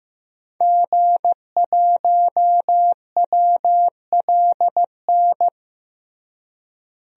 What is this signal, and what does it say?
Morse code, heard as a keyed tone, about 15 wpm, 700 Hz. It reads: G1WLN
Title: DXCC-100 Call Signs - 5 Characters - Repeated 3x - 15wpm